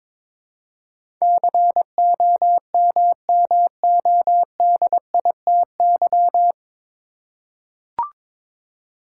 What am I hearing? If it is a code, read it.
Morse code, heard as a keyed tone, about 22 wpm, 700 Hz. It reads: COMMODITY
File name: 9 Letter Words - First 3 Letters Given - 22wpm